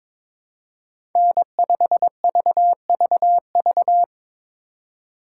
Morse code, heard as a keyed tone, about 22 wpm, 700 Hz. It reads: N5VVV
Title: USA Call Signs - 5 Characters - 3x Character Spacing - 22wpm